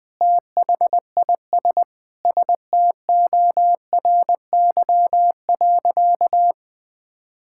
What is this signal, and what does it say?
Morse code, heard as a keyed tone, about 20 wpm, 700 Hz. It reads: THIS STORY.
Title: Sentences from Top 1600 Words 20wpm